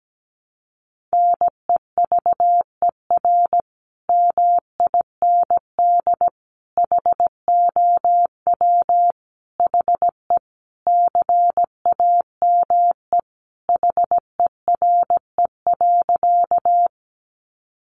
Morse code, heard as a keyed tone, about 17 wpm, 700 Hz. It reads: NEVER MIND HOW HE CAME HERE.
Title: Sentences from Top 500 Words 17wpm